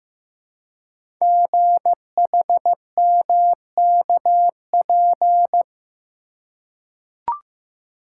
Morse code, heard as a keyed tone, about 15 wpm, 700 Hz. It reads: GHMKP
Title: Code Groups - 5 Letters - 15wpm